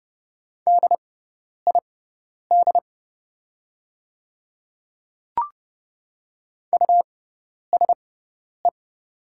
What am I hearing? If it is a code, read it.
Morse code, heard as a keyed tone, about 30 wpm, spaced as for 10 wpm, 700 Hz. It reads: DID USE
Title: 6x Character Spacing - Top 100 Words - 30wpm